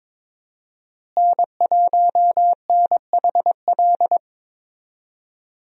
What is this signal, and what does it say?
Morse code, heard as a keyed tone, about 21 wpm, 700 Hz. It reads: N1NHL
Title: USA Call Signs - 5 Characters - Repeated 2x - 22wpm